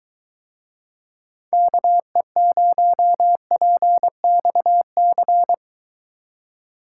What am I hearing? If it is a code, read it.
Morse code, heard as a keyed tone, about 23 wpm, 700 Hz. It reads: KE0PXC